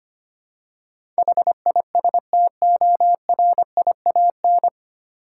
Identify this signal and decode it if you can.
Morse code, heard as a keyed tone, about 25 wpm, 700 Hz. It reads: HISTORIAN
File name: practice / 25wpm / 9 Letter Words - First 3 Letters Given - 25wpm